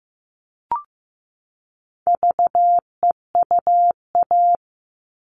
Morse code, heard as a keyed tone, about 15 wpm, 700 Hz. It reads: VEUA
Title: All Cut Numbers - 4-Digit 15wpm